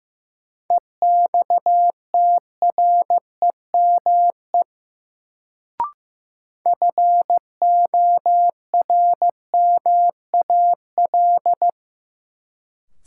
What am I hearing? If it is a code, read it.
Morse code, heard as a keyed tone, about 15 wpm, 700 Hz. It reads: EXTREME FORMAL
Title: Sentences from Top 1300 Words - New Word Drill Alt - 15wpm